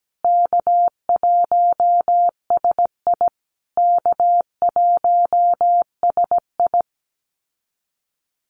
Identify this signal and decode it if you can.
Morse code, heard as a keyed tone, about 17 wpm, 700 Hz. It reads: K1SI K1SI
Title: USA Call Signs Repeated 3x 17wpm